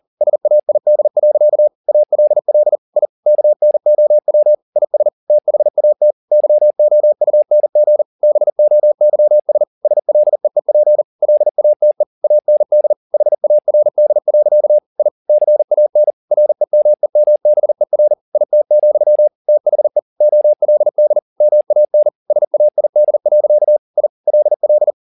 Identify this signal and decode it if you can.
Morse code, heard as a keyed tone, about 40 wpm, 600 Hz. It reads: SAID. ALL I KNOW IS THAT YOUNG BOYS SLEEP LATE AND HARD. I CAN REMEMBER IT, THE OLD MAN SAID. I LL